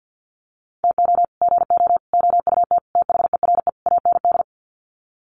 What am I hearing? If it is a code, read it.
Morse code, heard as a keyed tone, about 50 wpm, 700 Hz. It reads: TO GO OUT THERE AND
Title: Sets of 5 Words Encore2 50wpm